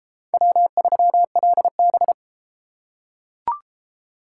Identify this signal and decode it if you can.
Morse code, heard as a keyed tone, about 30 wpm, 700 Hz. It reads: W3LB